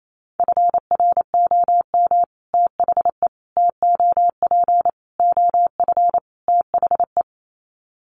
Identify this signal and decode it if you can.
Morse code, heard as a keyed tone, about 28 wpm, 700 Hz. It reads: FROM THE TOP OF THE
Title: Sets of 5 Words Encore 28wpm